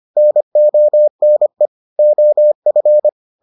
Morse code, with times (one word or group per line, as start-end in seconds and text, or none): 0.2-1.7 NONE
2.0-3.1 OF